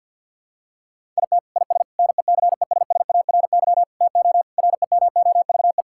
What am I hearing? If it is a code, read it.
Morse code, heard as a keyed tone, about 50 wpm, 700 Hz. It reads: IT IS NECESSARY TO REMOVE
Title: Sentences from Top 1000 Words - Review Part 5 - 50wpm